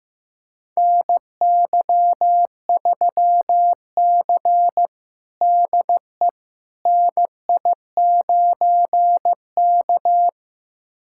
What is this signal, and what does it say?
Morse code, heard as a keyed tone, about 15 wpm, 700 Hz. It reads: NY3C DE NI9K
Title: USA Callsign DE Callsign - 4 Characters - 15wpm